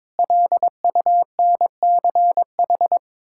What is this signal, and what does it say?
Morse code, heard as a keyed tone, about 22 wpm, 700 Hz. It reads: LUNCH
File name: Sets of 9 Words - 22wpm